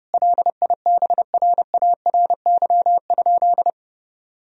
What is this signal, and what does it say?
Morse code, heard as a keyed tone, about 32 wpm, 700 Hz. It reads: LIBRARY?
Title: Sentences from Top 1000 Words - Review Part 4 - 30wpm